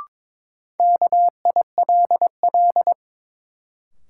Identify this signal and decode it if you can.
Morse code, heard as a keyed tone, about 22 wpm, 700 Hz. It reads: KILL
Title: Sentences from Top 1400 Words - New Word Drill - 22wpm